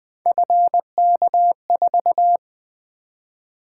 Morse code, heard as a keyed tone, about 20 wpm, 700 Hz. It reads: FK4
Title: Three Letter-Number Combinations - 20wpm